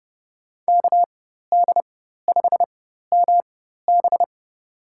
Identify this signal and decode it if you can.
Morse code, heard as a keyed tone, about 31 wpm, spaced as for 14 wpm, 700 Hz. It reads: KD5MB
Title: USA Call Signs - 4x Character Spacing - 30wpm